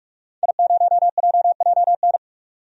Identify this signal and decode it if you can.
Morse code, heard as a keyed tone, about 48 wpm, 700 Hz. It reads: I0JJN
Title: DXCC-100 Call Signs - 5 Characters - 45wpm